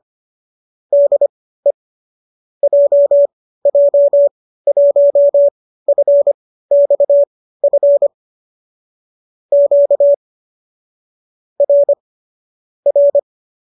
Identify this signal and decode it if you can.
Morse code, heard as a keyed tone, about 25 wpm, spaced as for 15 wpm, 570 Hz. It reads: DE JJ1FXF Q R R